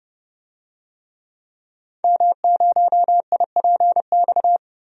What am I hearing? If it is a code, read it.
Morse code, heard as a keyed tone, about 30 wpm, 700 Hz. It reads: M0IPX
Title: DXCC-100 Call Signs - 5 Characters - Repeated 2x - 30wpm